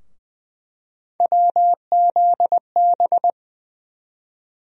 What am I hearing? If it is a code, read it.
Morse code, heard as a keyed tone, about 20 wpm, 700 Hz. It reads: WZB
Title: USA Call Sign Postfixes - 3 Letters 20wpm